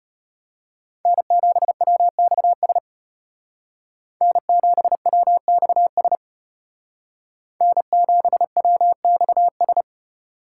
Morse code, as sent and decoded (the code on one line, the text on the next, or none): -. --... .-- -..- ... / -. --... .-- -..- ... / -. --... .-- -..- ...
N7WXS N7WXS N7WXS